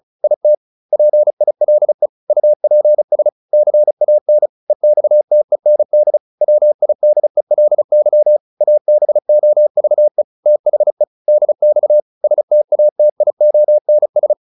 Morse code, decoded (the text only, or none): IT PILE UPS CAN EXTEND WIDELY ABOVE THE DX STATIONS